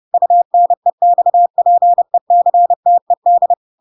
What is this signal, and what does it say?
Morse code, heard as a keyed tone, about 30 wpm, 700 Hz. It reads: UNEXPECTED
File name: Pinocchio / Pinocchio ch3 30 30 ik1hge